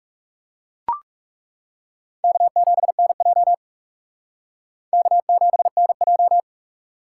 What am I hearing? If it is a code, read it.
Morse code, heard as a keyed tone, about 37 wpm, 700 Hz. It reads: K7NJ K7NJ
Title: Speed Racing - USA 1x2 Call Signs 30wpm